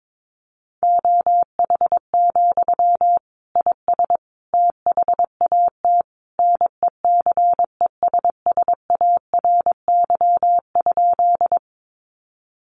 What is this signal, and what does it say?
Morse code, heard as a keyed tone, about 22 wpm, 700 Hz. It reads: OH, IS THAT NECESSARY?
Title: Sentences from Top 800 Words 22wpm